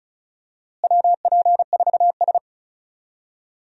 Morse code, heard as a keyed tone, about 36 wpm, 700 Hz. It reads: WP4S
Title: USA Call Signs - 4 Characters - Repeated 3x - 35wpm